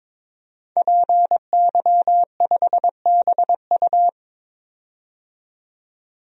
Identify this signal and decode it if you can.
Morse code, heard as a keyed tone, about 22 wpm, 700 Hz. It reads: PY5BU
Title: DXCC-100 Call Signs - 5 Characters - 22wpm